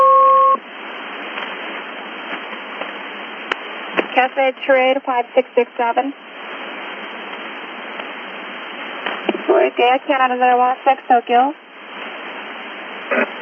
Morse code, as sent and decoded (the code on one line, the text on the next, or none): none